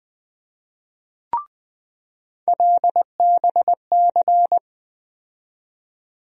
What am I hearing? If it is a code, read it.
Morse code, heard as a keyed tone, about 21 wpm, 700 Hz. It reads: LBC
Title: USA Call Sign Postfixes - 3 Letters 20wpm